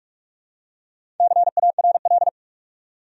Morse code, heard as a keyed tone, about 44 wpm, 700 Hz. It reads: CARL
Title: USA Common Names - 45wpm